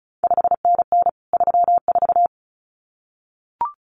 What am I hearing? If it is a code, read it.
Morse code, heard as a keyed tone, about 36 wpm, 700 Hz. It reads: 5NN 34
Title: CQ WW DX Contest Exchange Encore - 35wpm